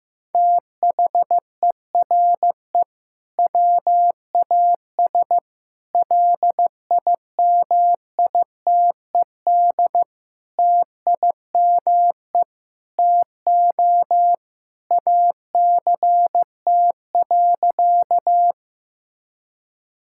Morse code, heard as a keyed tone, about 15 wpm, 700 Hz. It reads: THERE WAS LIMITED TIME TO ACT.
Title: Sentences from Top 1000 Words - Review Part 6 - 15wpm